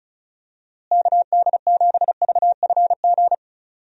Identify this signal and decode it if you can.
Morse code, heard as a keyed tone, about 36 wpm, 700 Hz. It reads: KD7VFG
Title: USA Call Signs - 6 Characters - Repeated 2x - 35wpm